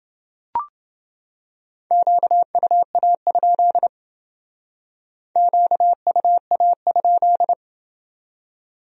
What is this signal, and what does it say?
Morse code, heard as a keyed tone, about 31 wpm, 700 Hz. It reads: QUA? QUA?